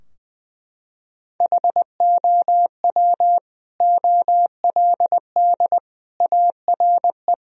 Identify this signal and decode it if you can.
Morse code, heard as a keyed tone, about 20 wpm, 700 Hz. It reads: HOW OLD ARE
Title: Sentences from Top 1000 Words - Review Part 4 - 20wpm